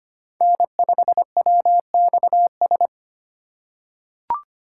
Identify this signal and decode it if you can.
Morse code, heard as a keyed tone, about 25 wpm, 700 Hz. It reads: N5WXS
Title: USA Call Signs - 5 Characters - 2x Character Spacing - 25wpm